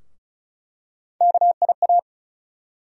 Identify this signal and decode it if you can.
Morse code, heard as a keyed tone, about 36 wpm, 700 Hz. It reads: KIA
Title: USA Call Sign Postfixes - 3 Letters 35wpm